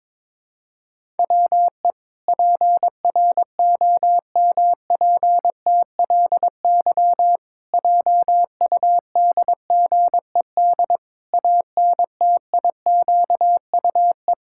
Morse code, heard as a keyed tone, about 22 wpm, 700 Hz. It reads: WE PROMPTLY JUDGED ANTIQUE
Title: Pangram Challenge v2 - 22wpm